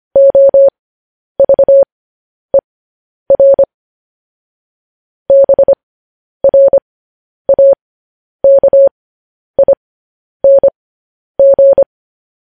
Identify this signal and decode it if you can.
Morse code, heard as a keyed tone, about 25 wpm, spaced as for 10 wpm, 550 Hz. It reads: OVER BRAKING
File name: podcast / News headlines-1x-25x10